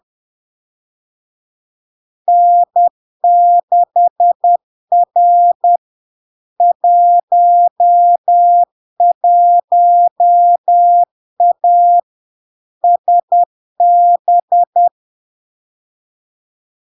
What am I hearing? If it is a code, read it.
Morse code, heard as a keyed tone, about 10 wpm, 700 Hz. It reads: N6R 11A SB
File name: LARK-FD-2022-Log-in-CW-10wpm-700-Hz